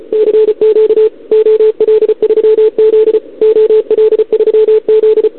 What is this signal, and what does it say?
Morse code, heard as a keyed tone, about 34 wpm, 440 Hz. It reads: CQ OL3Z OL3Z
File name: CQWWCW2021 / Doderna OL3Z